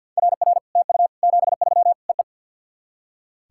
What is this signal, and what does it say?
Morse code, heard as a keyed tone, about 56 wpm, 700 Hz. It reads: RR TU 73 EE
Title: SOTA QSO Practice 50wpm